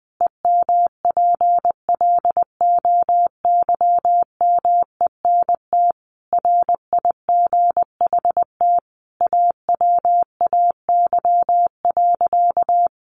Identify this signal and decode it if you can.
Morse code, heard as a keyed tone, about 19 wpm, 700 Hz. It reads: EMPLOYMENT RIGHT AWAY.
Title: Sentences from Top 1500 Words 20wpm